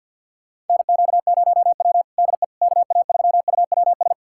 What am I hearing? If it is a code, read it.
Morse code, heard as a keyed tone, about 51 wpm, 700 Hz. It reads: NQ0W DE KA3UWS
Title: USA Callsign DE Callsign 50wpm